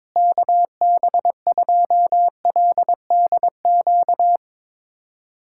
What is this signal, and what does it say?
Morse code, heard as a keyed tone, about 22 wpm, 700 Hz. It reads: KB2LDQ
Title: USA Callsign DE Callsign - 6 Characters - 22wpm